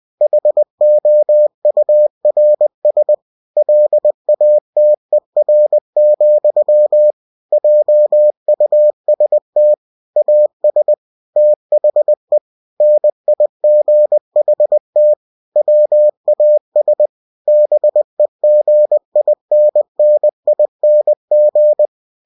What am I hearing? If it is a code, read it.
Morse code, heard as a keyed tone, about 20 wpm, 600 Hz. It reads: HOURS LATER, JUST AS THE NIGHT WAS BEGINNING
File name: cw-books / winnie-the-pooh / winnie-20wpm-page-0038